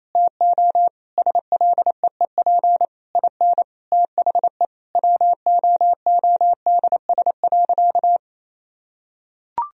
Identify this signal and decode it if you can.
Morse code, heard as a keyed tone, about 27 wpm, 690 Hz. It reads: TO SLEEP IN THE WOODS.
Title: Sentences from Top 1000 Words - Review Part 2 - 28wpm